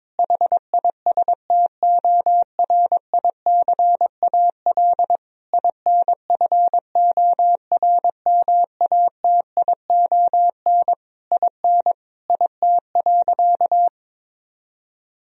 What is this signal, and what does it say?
Morse code, heard as a keyed tone, about 22 wpm, 700 Hz. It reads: HISTORICAL INFORMATION IN IT.
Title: Sentences from Top 1000 Words - Review Part 4 - 22wpm